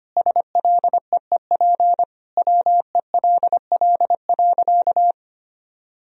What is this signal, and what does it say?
Morse code, heard as a keyed tone, about 25 wpm, 700 Hz. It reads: SLEEP WELL.
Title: Sentences from Top 500 Words and Abbreviations 25wpm